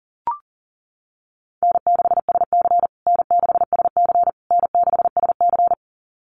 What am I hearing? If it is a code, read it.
Morse code, heard as a keyed tone, about 40 wpm, 700 Hz. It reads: N6SC N6SC N6SC
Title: USA Call Signs Repeated 3x 40wpm